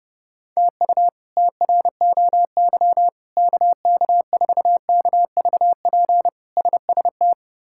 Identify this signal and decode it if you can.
Morse code, heard as a keyed tone, about 32 wpm, 700 Hz. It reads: TU TROY KK4KVP SST
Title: K1USN SST QSO Practice - 30wpm